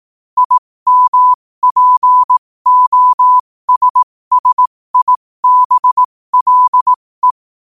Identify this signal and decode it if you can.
Morse code, heard as a keyed tone, about 18 wpm, spaced as for 15 wpm, 1000 Hz. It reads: IMPOSSIBLE
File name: qotdfiles / 15wpm